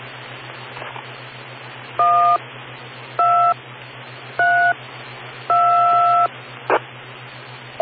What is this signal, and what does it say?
DTMF, heard as telephone touch tones, tones of about 370 ms. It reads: 1 2 3 2